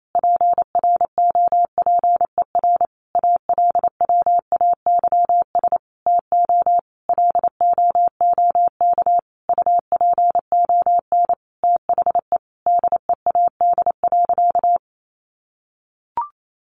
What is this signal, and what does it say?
Morse code, heard as a keyed tone, about 28 wpm, 700 Hz. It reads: PROPER ALWAYS TO LOOK UPON THE DEAD.
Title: Sentences from Top 1500 Words 28wpm